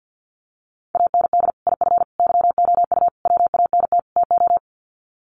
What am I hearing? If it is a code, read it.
Morse code, heard as a keyed tone, about 50 wpm, 700 Hz. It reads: AND IF YOU WANT TO
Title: Sets of 5 Words Encore 50wpm